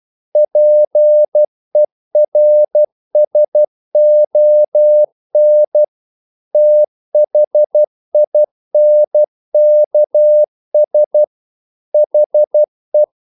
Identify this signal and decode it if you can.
Morse code, heard as a keyed tone, about 12 wpm, 600 Hz. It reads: PERSON THINKS HE
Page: English quotations in CW (telegraphy - morse)